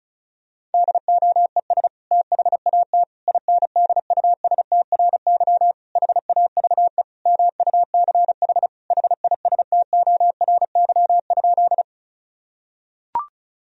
Morse code, heard as a keyed tone, about 36 wpm, 700 Hz. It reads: DOES THAT INDUSTRY HAVE MUCH HISTORY?
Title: Sentences from Top 1000 Words - Review Part 4 - 35wpm